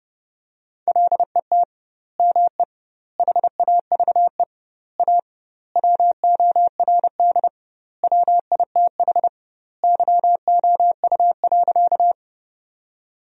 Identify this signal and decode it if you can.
Morse code, heard as a keyed tone, about 32 wpm, 700 Hz. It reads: LET ME HAVE A WORD WITH YOU.
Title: 2x Word-Spacing - Sentences from Top 300 Words - 30wpm